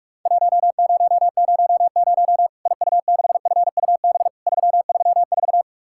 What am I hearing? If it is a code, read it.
Morse code, heard as a keyed tone, about 44 wpm, 700 Hz. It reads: 1000 IU6FUB 334
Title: Worked All Europe DX Contest - QTC Exchange - 45wpm